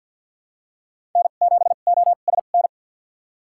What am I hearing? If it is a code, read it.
Morse code, heard as a keyed tone, about 53 wpm, spaced as for 37 wpm, 700 Hz. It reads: N7OSN
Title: USA Call Signs - 5 Characters - 2x Character Spacing - 50wpm